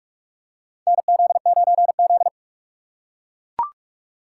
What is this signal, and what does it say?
Morse code, heard as a keyed tone, about 41 wpm, 700 Hz. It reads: NZ9Z
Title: USA Call Signs - 4 Characters - 2x Character Spacing - 45wpm